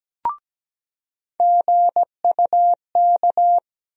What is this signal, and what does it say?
Morse code, heard as a keyed tone, about 17 wpm, 700 Hz. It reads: GUK
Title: USA Call Sign Postfixes - 3 Letters 17wpm